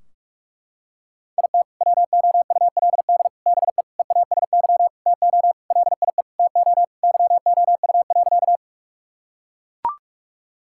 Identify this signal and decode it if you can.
Morse code, heard as a keyed tone, about 43 wpm, 700 Hz. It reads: IT WOULD BE EASY TO LIE TO YOU.
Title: Sentences from Top 1600 Words 45wpm